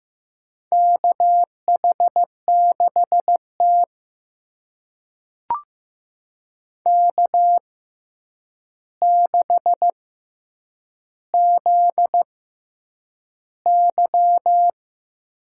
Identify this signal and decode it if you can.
Morse code, heard as a keyed tone, about 15 wpm, spaced as for 8 wpm, 700 Hz. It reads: KH6T K6ZY
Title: USA Call Signs - 4 Characters - 6x Character Spacing - 15wpm